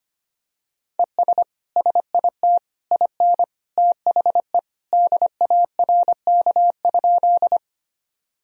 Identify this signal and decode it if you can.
Morse code, heard as a keyed tone, about 25 wpm, 700 Hz. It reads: ES SIT IN THE DARK?
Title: Sentences from Top 500 Words and Abbreviations 25wpm